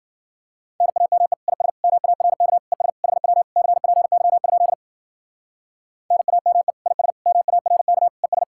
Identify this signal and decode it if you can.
Morse code, heard as a keyed tone, about 51 wpm, 700 Hz. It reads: NAME IS MARK ES HW CPY? NAME IS MARK ES